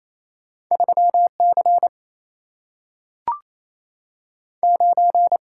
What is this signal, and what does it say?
Morse code, heard as a keyed tone, about 28 wpm, 700 Hz. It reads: 3 C 9
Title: Single-Letter Single-Number 28wpm